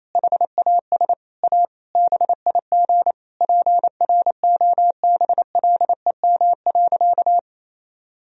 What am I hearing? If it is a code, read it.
Morse code, heard as a keyed tone, about 28 wpm, 700 Hz. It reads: HAS A BIG PROBLEM.
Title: Sentences from Top 1000 Words - Review Part 4 - 28wpm